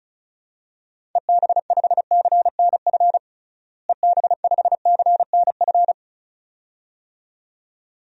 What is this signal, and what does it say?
Morse code, heard as a keyed tone, about 34 wpm, 700 Hz. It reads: EB5CNF EB5CNF
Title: DXCC-100 Call Signs - 6 Characters - Repeated 2x - 35wpm